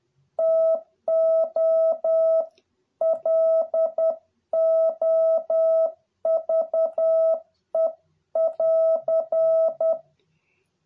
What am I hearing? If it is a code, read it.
Morse code, heard as a keyed tone, about 10 wpm, 650 Hz. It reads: TO LOVE <AR>